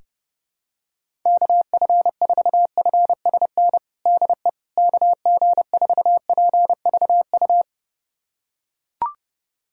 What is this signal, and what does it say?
Morse code, heard as a keyed tone, about 32 wpm, 700 Hz. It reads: KF4FSN DE KG4PVU
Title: USA Callsign DE Callsign - 6 Characters - 30wpm